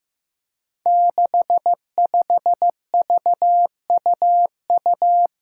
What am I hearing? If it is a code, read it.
Morse code, heard as a keyed tone, about 15 wpm, 700 Hz. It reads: 65VUU
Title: Five Challenging Letter-Number Combinations 15wpm